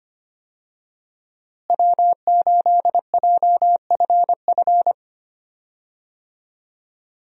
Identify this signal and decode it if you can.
Morse code, heard as a keyed tone, about 25 wpm, 700 Hz. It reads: W8JFF
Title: USA Call Signs - 5 Characters - Repeated 3x - 25wpm